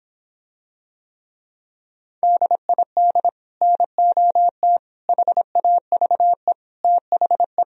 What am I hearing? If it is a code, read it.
Morse code, heard as a keyed tone, about 26 wpm, 700 Hz. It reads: DID NOT HAVE THE